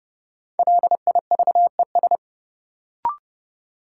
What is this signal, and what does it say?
Morse code, heard as a keyed tone, about 30 wpm, 700 Hz. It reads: LIVES
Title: Sets of 5 Words Encore - New Word Drill - 30wpm